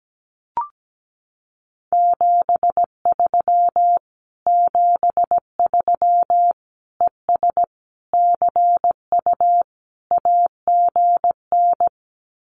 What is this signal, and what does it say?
Morse code, heard as a keyed tone, about 17 wpm, 700 Hz. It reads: 73 73 ES CU AGN
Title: Common QSO Phrases Encore 17wpm